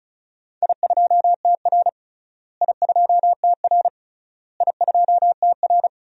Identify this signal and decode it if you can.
Morse code, heard as a keyed tone, about 36 wpm, 700 Hz. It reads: I2TR I2TR I2TR
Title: DXCC-100 Call Signs - 4 Characters - Repeated 3x - 35wpm